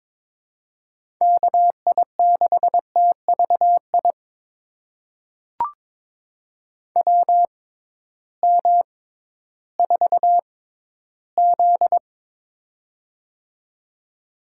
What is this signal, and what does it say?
Morse code, heard as a keyed tone, about 22 wpm, spaced as for 13 wpm, 700 Hz. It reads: KI6TVI WM4Z
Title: USA Call Signs - 6x Character Spacing - 22wpm